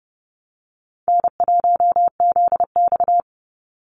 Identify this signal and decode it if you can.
Morse code, heard as a keyed tone, about 30 wpm, 700 Hz. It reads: N1ZX
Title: Easy Call Signs - 30wpm